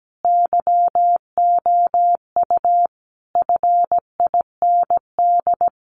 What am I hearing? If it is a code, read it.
Morse code, heard as a keyed tone, about 17 wpm, 700 Hz. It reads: YOU FIND
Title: Sentences from Top 100 Words 17wpm